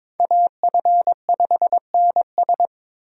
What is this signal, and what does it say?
Morse code, heard as a keyed tone, about 22 wpm, 700 Hz. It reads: AF5NS
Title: USA Call Signs - 5 Characters - Repeated 2x - 22wpm